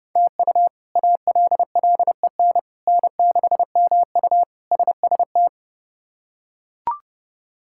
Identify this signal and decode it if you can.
Morse code, heard as a keyed tone, about 30 wpm, 700 Hz. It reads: TU ALLEN N6MU SST
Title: K1USN SST QSO Practice - 30wpm